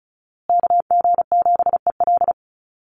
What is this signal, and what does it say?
Morse code, heard as a keyed tone, about 34 wpm, 700 Hz. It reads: KG7EL
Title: USA Call Signs Repeated 3x 35wpm